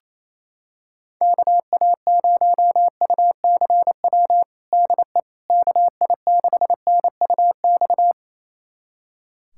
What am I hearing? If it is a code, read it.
Morse code, heard as a keyed tone, about 27 wpm, 700 Hz. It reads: KA0UCW DE KI6NUX
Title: USA Callsign DE Callsign - 6 Characters - 28wpm